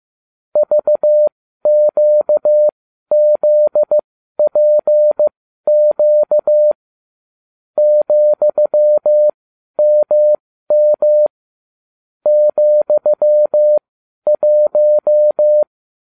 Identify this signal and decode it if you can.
Morse code, heard as a keyed tone, about 15 wpm, spaced as for 12 wpm, 600 Hz. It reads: VQZPQ ,MM ,1